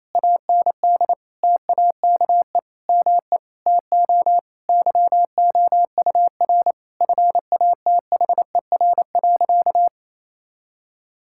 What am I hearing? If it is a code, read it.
Morse code, heard as a keyed tone, about 28 wpm, 700 Hz. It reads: AND TAKE ME TO YOUR FATHER.
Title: Sentences from Top 1000 Words - Review Part 6 - 28wpm